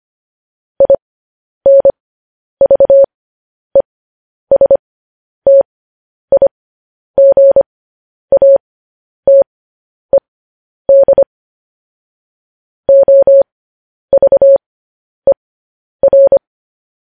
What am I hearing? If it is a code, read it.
Morse code, heard as a keyed tone, about 25 wpm, spaced as for 10 wpm, 550 Hz. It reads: INVESTIGATED OVER
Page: news-headlines-in-morse-code-at-10-wpm